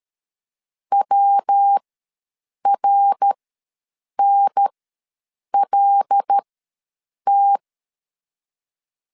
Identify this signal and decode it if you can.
Morse code, heard as a keyed tone, about 13 wpm, spaced as for 7 wpm, 800 Hz. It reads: WRNLT
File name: LEZIONI / 5 LBVC / 5 L ws 12wpm 2958